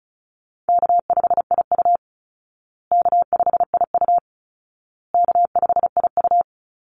Morse code, heard as a keyed tone, about 34 wpm, 700 Hz. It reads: K5IU K5IU K5IU
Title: Easy Call Signs Repeated 3x - 35wpm